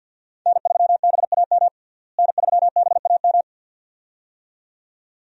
Morse code, heard as a keyed tone, about 50 wpm, 700 Hz. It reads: N3BAM N3BAM
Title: USA Call Signs - 5 Characters - Repeated 2x - 50wpm